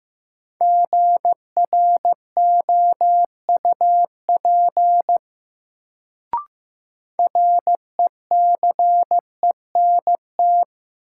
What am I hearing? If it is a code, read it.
Morse code, heard as a keyed tone, about 15 wpm, 700 Hz. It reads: GROUP RECENT
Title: Sets of 3 Words - New Word Drill - 15wpm